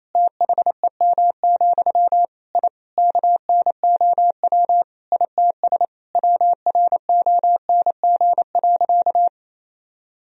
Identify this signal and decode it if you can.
Morse code, heard as a keyed tone, about 28 wpm, 700 Hz. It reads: THEM, I KNOW ITS WRONG.
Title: Sentences from Top 1000 Words - Review Part 1 - 28wpm